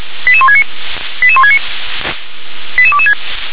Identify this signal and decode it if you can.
Five-tone selective call (ZVEI), heard as quick runs of five tones, tones of about 70 ms. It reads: 88169 88169 88226
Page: SDR/ZVEI